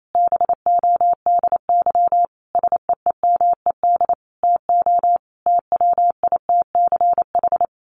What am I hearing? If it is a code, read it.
Morse code, heard as a keyed tone, about 28 wpm, 700 Hz. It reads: BODY SEEMED TO TWITCH
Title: A Sound Of Thunder 28wpm